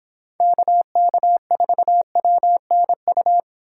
Morse code, heard as a keyed tone, about 25 wpm, 700 Hz. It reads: KK4WNU